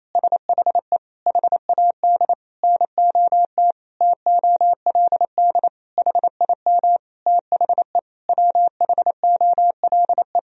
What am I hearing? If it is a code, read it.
Morse code, heard as a keyed tone, about 28 wpm, 700 Hz. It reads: SHE HAD NOT TOLD HIM THE WHOLE
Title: Sentences from Top 1000 Words - Review Part 1 - 28wpm